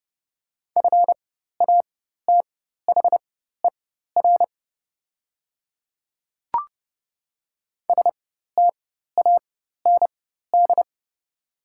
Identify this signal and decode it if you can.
Morse code, heard as a keyed tone, about 30 wpm, spaced as for 14 wpm, 690 Hz. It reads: FATHER STAND
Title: 4x Character Spacing - Top 300 Words - 30wpm